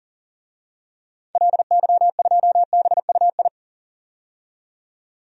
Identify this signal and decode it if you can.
Morse code, heard as a keyed tone, about 40 wpm, 700 Hz. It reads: LY2BUI
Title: DXCC-100 Call Signs - 6 Characters - 40wpm